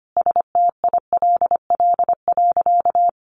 Morse code, heard as a keyed tone, about 25 wpm, 700 Hz. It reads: STILL.
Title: Sentences from Top 500 Words 25wpm